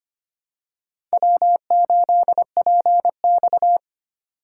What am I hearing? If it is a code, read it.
Morse code, heard as a keyed tone, about 25 wpm, 700 Hz. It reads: W8PX
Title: USA Call Signs - 4 Characters - 3x Character Spacing - 25wpm